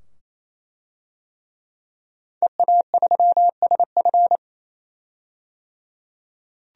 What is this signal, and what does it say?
Morse code, heard as a keyed tone, about 27 wpm, 700 Hz. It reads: EA3SF
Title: DXCC-100 Call Signs - 5 Characters - 28wpm